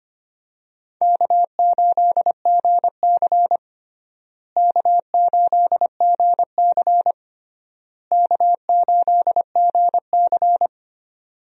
Morse code, heard as a keyed tone, about 25 wpm, 700 Hz. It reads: K8GC K8GC K8GC
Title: USA Call Signs - 4 Characters - Repeated 3x - 25wpm